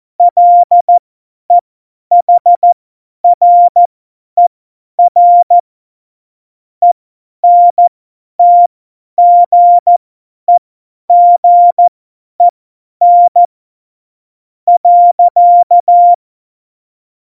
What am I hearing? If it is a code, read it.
Morse code, heard as a keyed tone, about 14 wpm, spaced as for 9 wpm, 700 Hz. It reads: LEHRER ENTGEGEN.